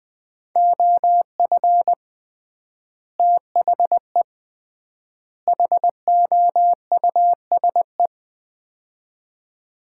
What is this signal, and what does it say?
Morse code, heard as a keyed tone, about 21 wpm, 700 Hz. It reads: OF THE HOUSE
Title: 3x Word-Spacing - Sets of 3 Words - 20wpm